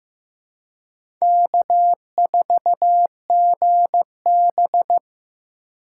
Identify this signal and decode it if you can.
Morse code, heard as a keyed tone, about 15 wpm, 700 Hz. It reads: K4GB
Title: USA Call Signs - 4x Character Spacing - 15wpm